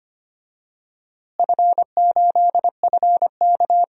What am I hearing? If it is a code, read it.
Morse code, heard as a keyed tone, about 25 wpm, 700 Hz. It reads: F8FK
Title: DXCC-100 Call Signs - 4 Characters - Repeated 2x - 25wpm